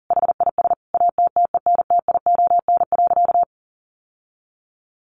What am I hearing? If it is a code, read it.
Morse code, heard as a keyed tone, about 37 wpm, 700 Hz. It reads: HIS ATTENTION.
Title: Sentences from Top 1400 Words 40wpm